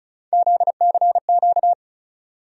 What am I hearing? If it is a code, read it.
Morse code, heard as a keyed tone, about 36 wpm, 700 Hz. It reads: ZCQ
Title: USA Call Sign Postfixes - 3 Letters 35wpm